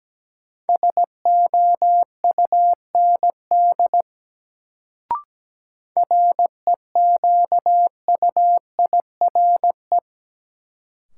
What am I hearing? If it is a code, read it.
Morse code, heard as a keyed tone, about 17 wpm, 700 Hz. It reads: SOUND REQUIRE
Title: Sentences from Top 900 Words - New Word Drill - 17wpm